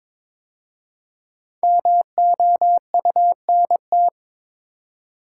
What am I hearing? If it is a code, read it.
Morse code, heard as a keyed tone, about 22 wpm, 690 Hz. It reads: MOUNT
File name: 5 Letter Words - First 3 Letters Given - 22wpm